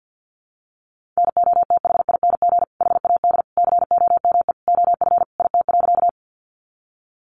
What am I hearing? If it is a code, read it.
Morse code, heard as a keyed tone, about 50 wpm, 700 Hz. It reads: NOTHING HAD COME OF IT.
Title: Sentences from Top 500 Words 50wpm